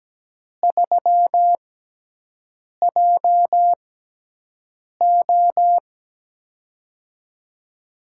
Morse code, heard as a keyed tone, about 17 wpm, spaced as for 6 wpm, 700 Hz. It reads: 3JO
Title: USA Call Signs - 4 Characters - 6x Character Spacing - 17wpm